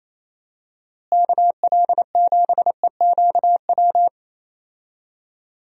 Morse code, heard as a keyed tone, about 28 wpm, 700 Hz. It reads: KL7EQW
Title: DXCC-100 Call Signs - 6 Characters - Repeated 3x - 28wpm